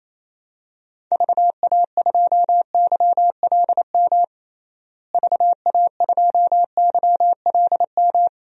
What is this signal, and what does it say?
Morse code, heard as a keyed tone, about 28 wpm, 700 Hz. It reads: VA2YLM VA2YLM
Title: DXCC-100 Call Signs - 6 Characters - Repeated 2x - 28wpm